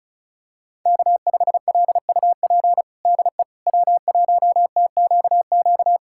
Morse code, heard as a keyed tone, about 36 wpm, 700 Hz. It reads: K5LUP DE W1TQQ
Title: USA Callsign DE Callsign - 5 Characters - 35wpm